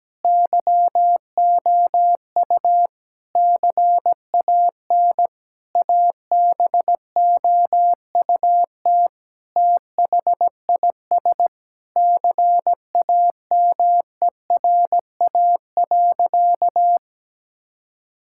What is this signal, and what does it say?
Morse code, heard as a keyed tone, about 17 wpm, 700 Hz. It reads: YOU CAN ABOUT THIS CAMERA.
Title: Sentences from Top 1000 Words - Review Part 5 - 17wpm